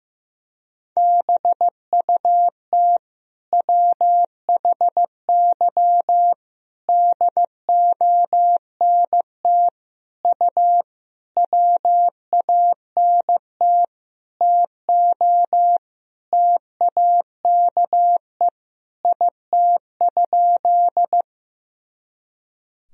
Morse code, heard as a keyed tone, about 15 wpm, 700 Hz. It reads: BUT WHY DONT U WANT TO TAKE IT?
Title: Sentences from Top 300 Words and Abbreviations 15wpm